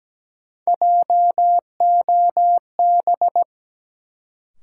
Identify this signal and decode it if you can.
Morse code, heard as a keyed tone, about 17 wpm, 700 Hz. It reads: JOB
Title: Sentences from Top 600 Words - New Word Drill - 17wpm